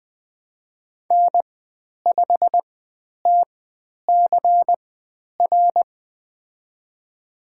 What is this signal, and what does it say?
Morse code, heard as a keyed tone, about 20 wpm, spaced as for 10 wpm, 700 Hz. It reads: N5TCR
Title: 102 Lesson 17g - Letter u - Call Signs 20wpm